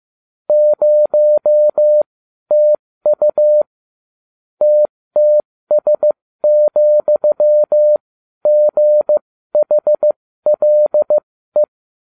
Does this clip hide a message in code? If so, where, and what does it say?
Yes: Morse code from 0.5 to 11.6 s, heard as a keyed tone, about 15 wpm, spaced as for 12 wpm, 600 Hz. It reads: 0TU TTS,GHLE